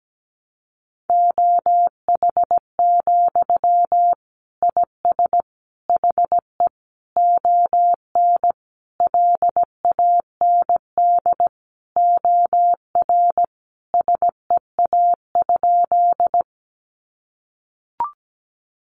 Morse code, heard as a keyed tone, about 17 wpm, 700 Hz. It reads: OH, IS HE ON LAND OR SEA?
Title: Sentences from Top 500 Words 17wpm